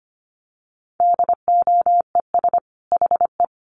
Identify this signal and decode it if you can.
Morse code, heard as a keyed tone, about 25 wpm, 700 Hz. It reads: DOES HE
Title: Sentences from Top 1300 Words 25wpm